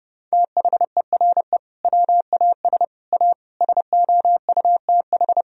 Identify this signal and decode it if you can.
Morse code, heard as a keyed tone, about 32 wpm, 700 Hz. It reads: THERE WAS A SOUTH
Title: Sentences from Top 1000 Words - Review Part 2 - 30wpm